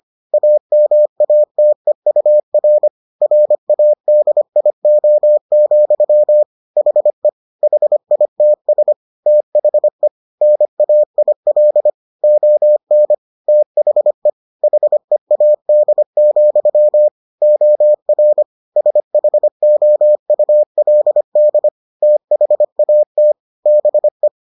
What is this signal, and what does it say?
Morse code, heard as a keyed tone, about 25 wpm, 600 Hz. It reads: AMATEUR RADIO, HE HITS THE NAIL ON THE HEAD, OR SHOULD THAT BE